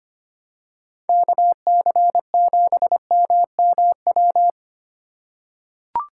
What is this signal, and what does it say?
Morse code, heard as a keyed tone, about 25 wpm, 700 Hz. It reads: KC7MMW
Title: USA Call Signs - 6 Characters - 2x Character Spacing - 25wpm